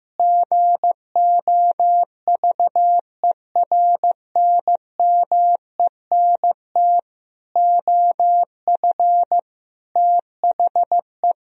Morse code, heard as a keyed tone, about 15 wpm, 700 Hz. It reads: GOVERNMENT OF THE
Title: Sentences from Top 1000 Words - Review Part 3 - 15wpm